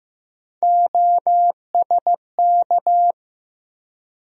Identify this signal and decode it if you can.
Morse code, heard as a keyed tone, about 15 wpm, 700 Hz. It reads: OSK
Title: USA Call Sign Postfixes - 3 Letters 15wpm